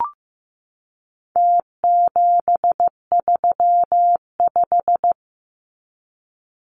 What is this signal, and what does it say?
Morse code, heard as a keyed tone, about 15 wpm, 700 Hz. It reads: T735
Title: Numbers - Contest Style - 4-Digit 15wpm